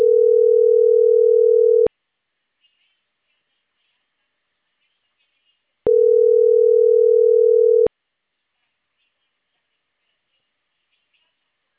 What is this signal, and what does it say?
Ringback tone (North America), 440 + 480 Hz.